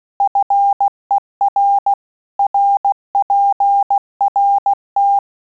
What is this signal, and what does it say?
Morse code, heard as a keyed tone, about 16 wpm, 800 Hz. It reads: FER RPRT
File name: Telegraphy-group / QSO CW 15 PPM